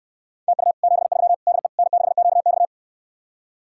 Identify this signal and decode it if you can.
Morse code, heard as a keyed tone, about 69 wpm, 700 Hz. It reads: TU 73 DE N6QX